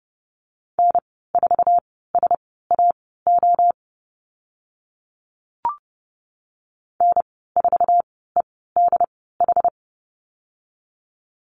Morse code, heard as a keyed tone, about 30 wpm, spaced as for 17 wpm, 700 Hz. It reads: N4SAO N4EDH
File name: Lesson 15g - Number 5 - Call Signs 30wpm